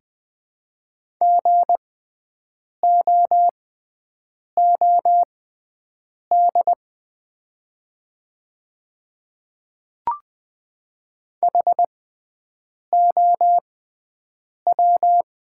Morse code, heard as a keyed tone, about 20 wpm, spaced as for 7 wpm, 700 Hz. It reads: GOOD HOW
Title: 6x Character Spacing - Top 300 Words - 20wpm